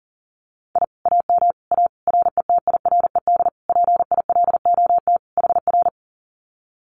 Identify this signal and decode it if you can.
Morse code, heard as a keyed tone, about 40 wpm, 700 Hz. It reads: I AM A RETIRED PILOT HR
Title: Common QSO Phrases Encore 40wpm